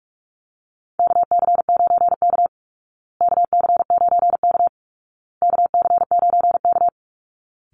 Morse code, heard as a keyed tone, about 44 wpm, 700 Hz. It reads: KC9K KC9K KC9K
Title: Call Signs Repeated 2x - 45wpm